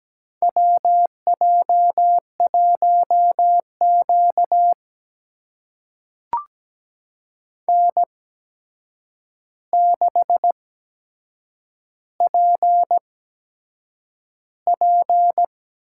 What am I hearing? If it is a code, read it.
Morse code, heard as a keyed tone, about 17 wpm, spaced as for 7 wpm, 700 Hz. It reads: WJ1Q N6PP
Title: USA Call Signs - 4 Characters - 8x Character Spacing - 17wpm